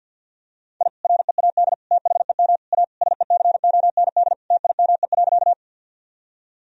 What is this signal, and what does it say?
Morse code, heard as a keyed tone, about 52 wpm, 700 Hz. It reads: I READ THEM A SECOND TIME.
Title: Sentences from Top 1000 Words - Review Part 1 - 50wpm